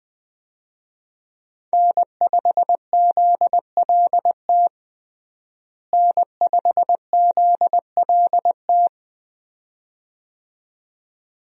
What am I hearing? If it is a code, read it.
Morse code, heard as a keyed tone, about 21 wpm, 700 Hz. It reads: N5ZLT N5ZLT
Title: USA Call Signs - 5 Characters - Repeated 2x - 20wpm